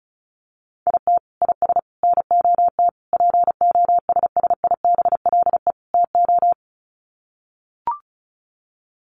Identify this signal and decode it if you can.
Morse code, heard as a keyed tone, about 34 wpm, 700 Hz. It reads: IT IS NOT POSSIBLE TO
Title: Sets of 5 Words Encore2 35wpm